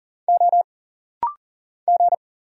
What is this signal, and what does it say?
Morse code, heard as a keyed tone, about 39 wpm, 700 Hz. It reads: O G
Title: Single Letter-Number Alt - Rapid-Fire 40wpm